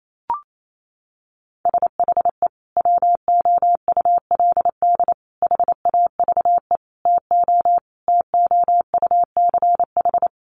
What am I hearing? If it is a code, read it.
Morse code, heard as a keyed tone, about 28 wpm, 700 Hz. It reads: SHE WOULD HAVE TO TOUCH
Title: Sentences from Top 1200 Words 28wpm